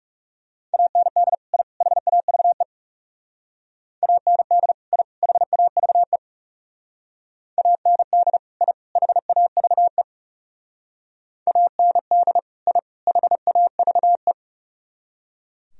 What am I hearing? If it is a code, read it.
Morse code, heard as a keyed tone, about 38 wpm, 700 Hz. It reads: AND I HAVE AND I HAVE AND I HAVE AND I HAVE